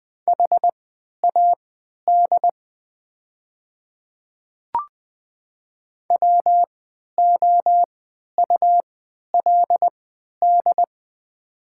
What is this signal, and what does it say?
Morse code, heard as a keyed tone, about 20 wpm, spaced as for 11 wpm, 700 Hz. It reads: HAD WOULD
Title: 3x Character Spacing - Top 300 Words - 20wpm